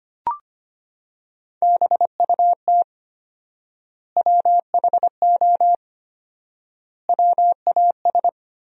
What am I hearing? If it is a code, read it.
Morse code, heard as a keyed tone, about 25 wpm, 700 Hz. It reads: BUT WHO WAS
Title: 4x Word-Spacing - Sentences from Top 200 Words - 25wpm